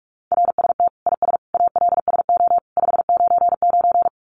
Morse code, heard as a keyed tone, about 44 wpm, 700 Hz. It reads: RST IS ALSO 599
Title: Common QSO Phrases 45wpm